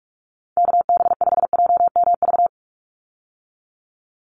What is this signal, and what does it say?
Morse code, heard as a keyed tone, about 44 wpm, 700 Hz. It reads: KB5JMV
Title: USA Call Signs 45wpm